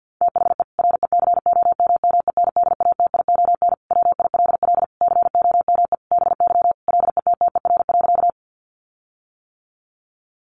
Morse code, heard as a keyed tone, about 50 wpm, 700 Hz. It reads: THE RECOMMENDATION WILL COME BY LETTER.
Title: Sentences from Top 1400 Words 50wpm